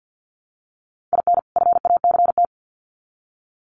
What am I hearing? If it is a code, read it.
Morse code, heard as a keyed tone, about 50 wpm, 700 Hz. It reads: IN FACT
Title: Sets of 2 Words 50wpm